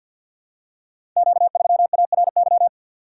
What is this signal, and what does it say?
Morse code, heard as a keyed tone, about 51 wpm, 700 Hz. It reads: Q3ARY
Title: Code Groups - 5 Letter:Numbers - 50wpm